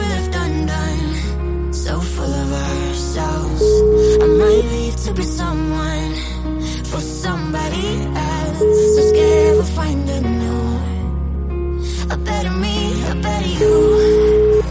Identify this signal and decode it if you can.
Ringback tone (Europe, 425 Hz), with noise.